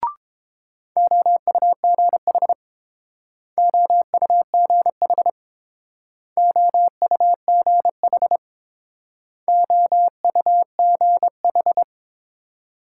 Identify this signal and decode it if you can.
Morse code, heard as a keyed tone, about 28 wpm, 700 Hz. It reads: OUGH OUGH OUGH OUGH
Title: Speed-Racing Multi-letter Phonemes - 22wpm